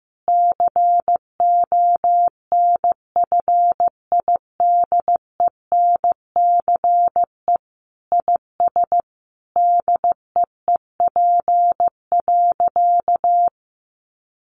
Morse code, heard as a keyed tone, about 15 wpm, 700 Hz. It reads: CONFIDENCE IS DEEP.
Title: Sentences from Top 1500 Words 15wpm